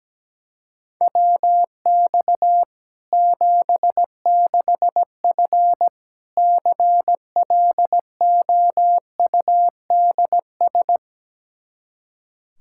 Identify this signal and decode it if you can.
Morse code, heard as a keyed tone, about 17 wpm, 700 Hz. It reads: WX 76F CLOUDS
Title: Practice QSO - Exchange 2 - Info - 17wpm